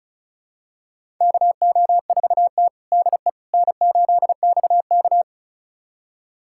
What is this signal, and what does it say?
Morse code, heard as a keyed tone, about 36 wpm, 700 Hz. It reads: KO4T DE N8XK
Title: USA Callsign DE Callsign - 4 Characters - 35wpm